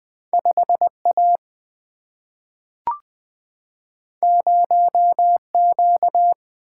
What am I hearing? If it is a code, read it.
Morse code, heard as a keyed tone, about 21 wpm, 700 Hz. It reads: 5A 0Q
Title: Single-Letter Single-Number 20wpm